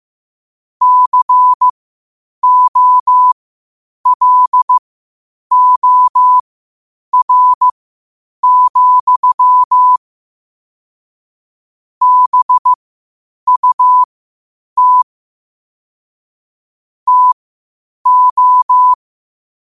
Morse code, heard as a keyed tone, about 15 wpm, spaced as for 8 wpm, 1000 Hz. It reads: COLOR, BUT TO
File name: qotdfiles / 8wpm